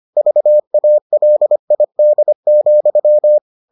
Morse code, heard as a keyed tone, about 25 wpm, 600 Hz. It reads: VALID,